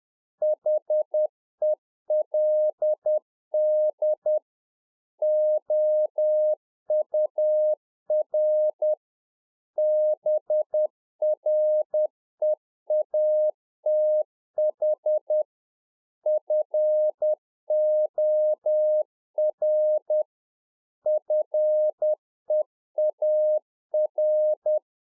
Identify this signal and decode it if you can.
Morse code, heard as a keyed tone, about 10 wpm, 610 Hz. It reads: HELD OUR BREATH FOR FEAR